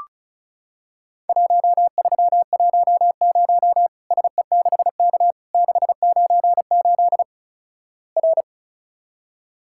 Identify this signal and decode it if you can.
Morse code, heard as a keyed tone, about 36 wpm, 700 Hz. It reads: 1310 SE6K 698 R
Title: Worked All Europe DX Contest - QTC Exchange - 35wpm